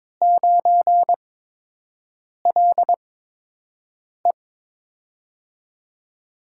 Morse code, heard as a keyed tone, about 22 wpm, spaced as for 6 wpm, 700 Hz. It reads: 9LE
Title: USA Call Signs - 5 Characters - 8x Character Spacing - 22wpm